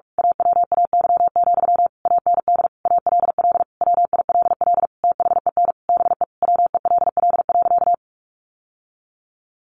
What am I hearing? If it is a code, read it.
Morse code, heard as a keyed tone, about 44 wpm, 700 Hz. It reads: AWAY, AND ALL WILL THEN BE WELL.
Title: Sentences from Top 1500 Words 45wpm